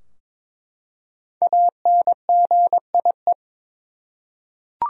Morse code, heard as a keyed tone, about 22 wpm, 700 Hz. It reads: ANGIE
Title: USA Common Names - Max 5 Letters - 22wpm